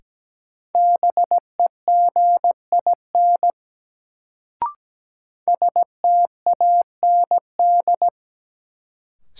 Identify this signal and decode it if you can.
Morse code, heard as a keyed tone, about 17 wpm, 700 Hz. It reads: BEGIN STAND
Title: Sentences from Top 300 Words - New Word Drill - 17wpm